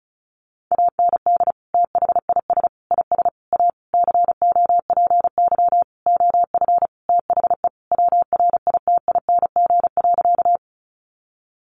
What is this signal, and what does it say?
Morse code, heard as a keyed tone, about 34 wpm, 700 Hz. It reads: AND THIS IS A COPY OF THE WRITING.
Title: Sentences from Top 1400 Words 35wpm